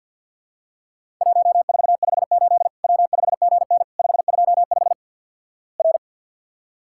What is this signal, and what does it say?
Morse code, heard as a keyed tone, about 54 wpm, 700 Hz. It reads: 1458 W5GN 525 R
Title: Worked All Europe DX Contest - QTC Exchange - 50wpm